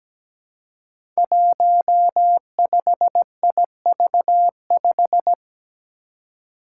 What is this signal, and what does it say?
Morse code, heard as a keyed tone, about 17 wpm, 700 Hz. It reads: 15IV5
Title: Five Challenging Letter-Number Combinations 17wpm